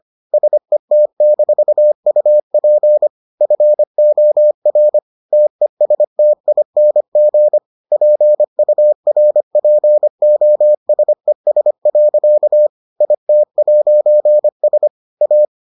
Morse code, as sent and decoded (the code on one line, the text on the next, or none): ... - -....- ..- .--. / ..-. --- .-. / - . ... - .. -. --. / .--. ..- .-. .--. --- ... ... .-.-.- / .. - .----. ... / .-
SET-UP FOR TESTING PURPOSES. IT'S A